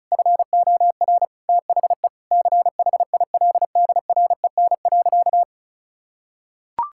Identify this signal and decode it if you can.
Morse code, heard as a keyed tone, about 34 wpm, 700 Hz. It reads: FOR THE CHILDREN.
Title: Sentences from Top 1000 Words - Review Part 6 - 35wpm